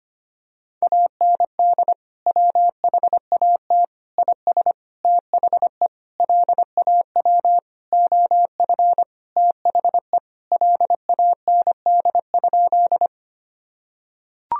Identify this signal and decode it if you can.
Morse code, heard as a keyed tone, about 25 wpm, 700 Hz. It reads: AND WHAT IS THE LAW OF THE LAND?
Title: Sentences from Top 1000 Words - Review Part 3 - 25wpm